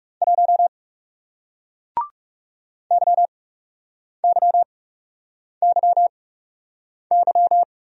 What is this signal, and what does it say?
Morse code, heard as a keyed tone, about 40 wpm, spaced as for 8 wpm, 700 Hz. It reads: 1 YYYY